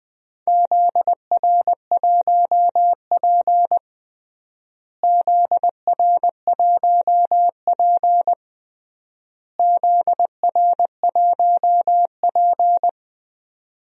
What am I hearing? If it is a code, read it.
Morse code, heard as a keyed tone, about 20 wpm, 700 Hz. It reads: ZR1P ZR1P ZR1P
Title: DXCC-100 Call Signs - 4 Characters - Repeated 3x - 20wpm